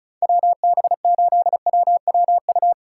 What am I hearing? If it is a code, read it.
Morse code, heard as a keyed tone, about 42 wpm, 700 Hz. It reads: WB8WWU
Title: Speed Racing - USA 2x3 Call Signs 35wpm